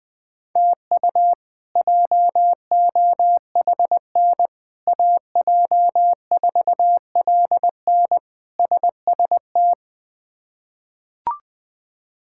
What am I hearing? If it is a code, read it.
Morse code, heard as a keyed tone, about 20 wpm, 700 Hz. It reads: TU JOHN AJ4LN SST
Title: K1USN SST QSO Practice - 20wpm